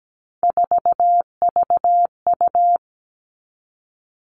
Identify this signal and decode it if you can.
Morse code, heard as a keyed tone, about 17 wpm, 700 Hz. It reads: 4VU
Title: All Cut Numbers - 3-Digit 17wpm